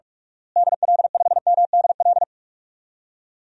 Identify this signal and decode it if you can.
Morse code, heard as a keyed tone, about 46 wpm, 700 Hz. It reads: DL5MDL
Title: DXCC-100 Call Signs - 6 Characters - Repeated 2x - 45wpm